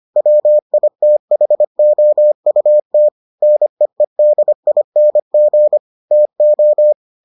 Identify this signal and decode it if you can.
Morse code, heard as a keyed tone, about 25 wpm, 600 Hz. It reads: WITHOUT NEEDING TO